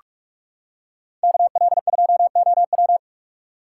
Morse code, heard as a keyed tone, about 39 wpm, 700 Hz. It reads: KL2OW